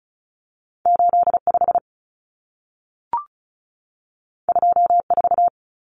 Morse code, heard as a keyed tone, about 36 wpm, 700 Hz. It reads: 85 24
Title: Common Cut Numbers - 2-Digit 35wpm